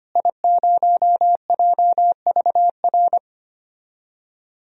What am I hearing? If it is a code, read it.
Morse code, heard as a keyed tone, about 25 wpm, 700 Hz. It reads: I0JVR
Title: DXCC-100 Call Signs - 5 Characters - Repeated 3x - 25wpm